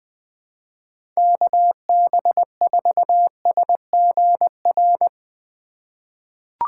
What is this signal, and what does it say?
Morse code, heard as a keyed tone, about 20 wpm, 700 Hz. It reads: KB4SGR
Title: USA Call Signs - 6 Characters - 4x Character Spacing - 20wpm